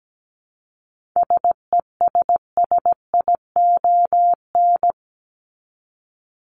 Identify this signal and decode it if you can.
Morse code, heard as a keyed tone, about 17 wpm, 700 Hz. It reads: SESSION
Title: Top 1501-1750 Words 17wpm